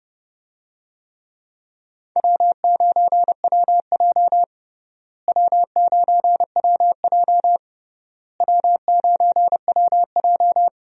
Morse code, heard as a keyed tone, about 32 wpm, 700 Hz. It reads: W9WJ W9WJ W9WJ
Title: USA Call Signs - 4 Characters - Repeated 3x - 30wpm